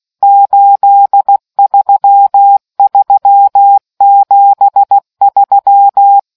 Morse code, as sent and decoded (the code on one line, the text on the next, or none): ---.. ...-- ...-- --... ...--
83373